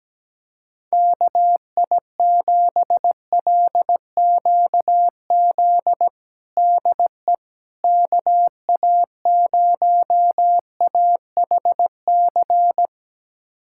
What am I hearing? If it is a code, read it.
Morse code, heard as a keyed tone, about 17 wpm, 700 Hz. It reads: KI7LQZ DE KA0AHC
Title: USA Callsign DE Callsign 17wpm